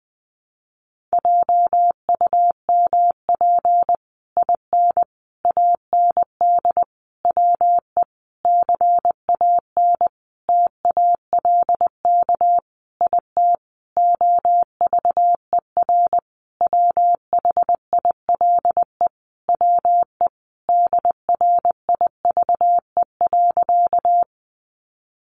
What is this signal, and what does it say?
Morse code, heard as a keyed tone, about 20 wpm, 700 Hz. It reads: JUMP IN AND WE CAN TALK IT OVER WHILE WE DRIVE.
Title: Sentences from Top 1300 Words 20wpm